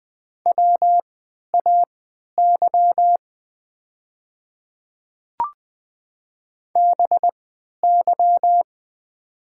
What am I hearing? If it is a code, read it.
Morse code, heard as a keyed tone, about 20 wpm, spaced as for 11 wpm, 700 Hz. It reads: WAY BY
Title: 3x Character Spacing - Top 100 Words - 20wpm